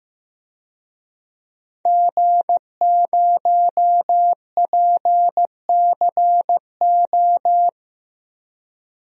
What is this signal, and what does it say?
Morse code, heard as a keyed tone, about 15 wpm, 700 Hz. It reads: G0PCO
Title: DXCC-100 Call Signs - 5 Characters - Repeated 3x - 15wpm